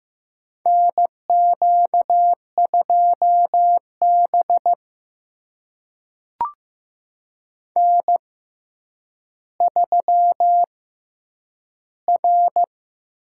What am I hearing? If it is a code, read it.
Morse code, heard as a keyed tone, about 15 wpm, spaced as for 9 wpm, 700 Hz. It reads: NQ2B N3R
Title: USA Call Signs - 4 Characters - 6x Character Spacing - 15wpm